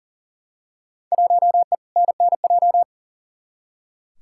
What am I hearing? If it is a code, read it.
Morse code, heard as a keyed tone, about 37 wpm, 700 Hz. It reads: 1E NNJ
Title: Field Day Exchanges - 40wpm